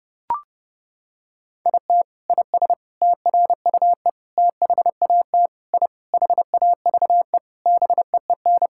Morse code, heard as a keyed tone, about 32 wpm, 700 Hz. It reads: IT IS TRUE THAT I HAVE BEEN
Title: Sentences from Top 1000 Words - Review Part 6 - 30wpm